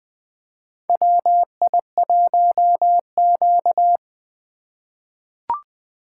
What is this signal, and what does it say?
Morse code, heard as a keyed tone, about 20 wpm, 700 Hz. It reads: WI1Q
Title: USA Call Signs - 3x Character Spacing - 20wpm